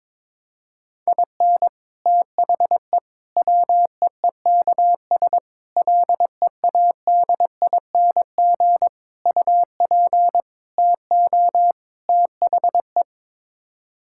Morse code, heard as a keyed tone, about 22 wpm, 700 Hz. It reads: IN THE WEEKS LEADING UP TO THE
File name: practice / 22wpm / Sets of 7 Words - 22wpm